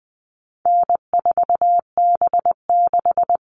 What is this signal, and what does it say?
Morse code, heard as a keyed tone, about 20 wpm, 700 Hz. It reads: N4B6
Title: All Cut Numbers - 4-Digit 20wpm